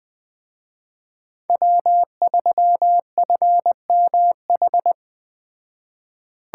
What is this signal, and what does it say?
Morse code, heard as a keyed tone, about 19 wpm, 700 Hz. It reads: W3FMH